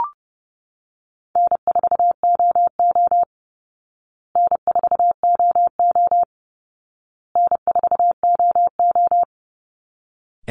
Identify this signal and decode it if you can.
Morse code, heard as a keyed tone, about 30 wpm, 700 Hz. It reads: N4OO N4OO N4OO
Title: Call Signs Repeated 2x - 30wpm